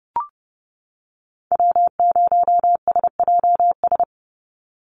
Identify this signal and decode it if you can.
Morse code, heard as a keyed tone, about 32 wpm, 700 Hz. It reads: W0SJS
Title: Call Signs - 30wpm